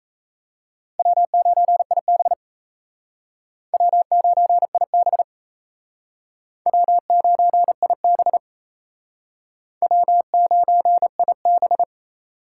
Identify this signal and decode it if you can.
Morse code, heard as a keyed tone, about 36 wpm, 700 Hz. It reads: W9IB W9IB W9IB W9IB